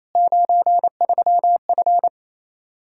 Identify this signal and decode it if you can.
Morse code, heard as a keyed tone, about 28 wpm, 700 Hz. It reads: 93F
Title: K1USN SST QSO Practice 28wpm